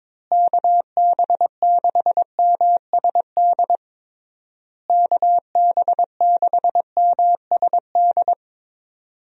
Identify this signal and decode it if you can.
Morse code, heard as a keyed tone, about 22 wpm, 700 Hz. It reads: KB6MSD KB6MSD
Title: USA Call Signs - 6 Characters - Repeated 3x - 22wpm